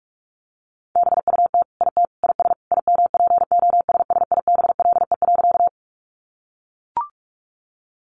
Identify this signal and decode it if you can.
Morse code, heard as a keyed tone, about 44 wpm, 700 Hz. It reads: BUT IT IS IMPOSSIBLE.
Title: Sentences from Top 1100 Words 45wpm